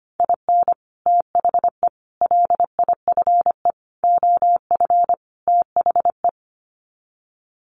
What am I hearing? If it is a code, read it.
Morse code, heard as a keyed tone, about 25 wpm, 700 Hz. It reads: IN THE LIFE OF THE
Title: Sets of 5 Words Encore2 25wpm